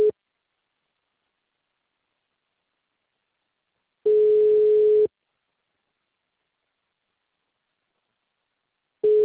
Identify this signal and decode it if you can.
Ringback tone (Europe, 425 Hz).